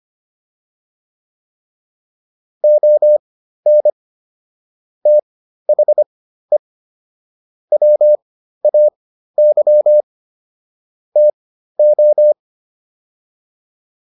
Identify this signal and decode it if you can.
Morse code, heard as a keyed tone, about 25 wpm, spaced as for 13 wpm, 600 Hz. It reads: ON THE WAY TO
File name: WD302 13